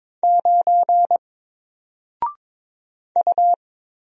Morse code, heard as a keyed tone, about 22 wpm, 700 Hz. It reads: 9 U
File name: Single-Letter-Number-22wpm